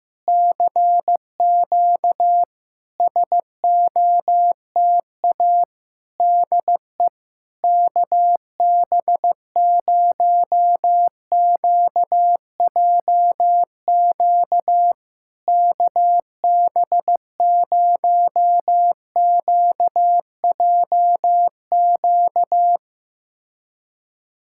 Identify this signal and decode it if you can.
Morse code, heard as a keyed tone, about 16 wpm, 700 Hz. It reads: CQ SOTA DE KB0QJQ KB0QJQ
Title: SOTA QSO Practice 15wpm